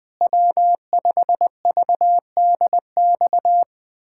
Morse code, heard as a keyed tone, about 20 wpm, 700 Hz. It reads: W5VDX
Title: USA Callsign DE Callsign 20wpm